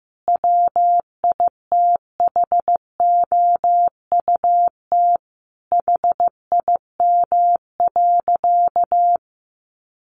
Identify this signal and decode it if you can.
Morse code, heard as a keyed tone, about 15 wpm, 700 Hz. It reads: WITHOUT HIM.
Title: Sentences from Top 1400 Words 15wpm